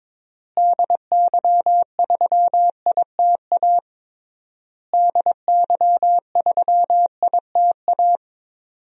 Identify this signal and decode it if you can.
Morse code, heard as a keyed tone, about 22 wpm, 700 Hz. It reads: DY3ITA DY3ITA
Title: DXCC-100 Call Signs - 6 Characters - Repeated 3x - 22wpm